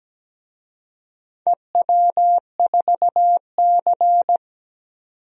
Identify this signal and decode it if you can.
Morse code, heard as a keyed tone, about 17 wpm, 700 Hz. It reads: EW4C
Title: DXCC-100 Call Signs - 4 Characters - 17wpm